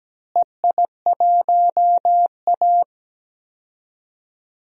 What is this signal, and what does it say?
Morse code, heard as a keyed tone, about 17 wpm, 700 Hz. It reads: EI1A
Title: DXCC-100 Call Signs - 4 Characters - Repeated 3x - 17wpm